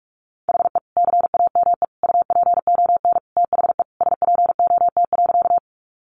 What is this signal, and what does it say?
Morse code, heard as a keyed tone, about 44 wpm, 700 Hz. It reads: HE CAME UPON THE SPOT.
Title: Sentences from Top 1100 Words 45wpm